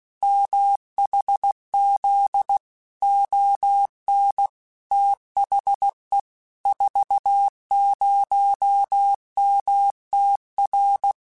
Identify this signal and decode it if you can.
Morse code, heard as a keyed tone, about 16 wpm, 800 Hz. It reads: MHZ ON THE 40MTR
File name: QSOs / QSOs-mp3s / kc7sac@15